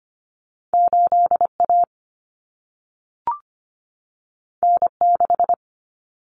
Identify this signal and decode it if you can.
Morse code, heard as a keyed tone, about 25 wpm, 700 Hz. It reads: 8A N6
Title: Common Cut Numbers - 2-Digit 25wpm